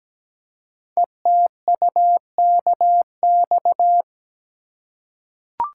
Morse code, heard as a keyed tone, about 17 wpm, 700 Hz. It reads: ETUKX
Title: Code Groups - 5 Letters - 17wpm